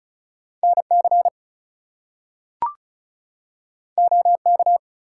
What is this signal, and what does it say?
Morse code, heard as a keyed tone, about 36 wpm, 700 Hz. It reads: NC OK
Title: US States and Canadian Provinces 35wpm